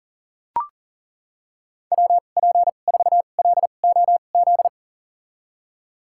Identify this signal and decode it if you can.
Morse code, heard as a keyed tone, about 40 wpm, spaced as for 29 wpm, 700 Hz. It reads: WP4LOZ
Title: USA Call Signs - 6 Characters - 2x Character Spacing - 40wpm